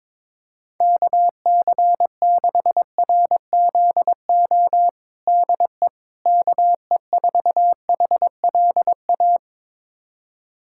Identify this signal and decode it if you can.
Morse code, heard as a keyed tone, about 22 wpm, 700 Hz. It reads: KC6RZO DE KE4HLA
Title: USA Callsign DE Callsign - 6 Characters - 22wpm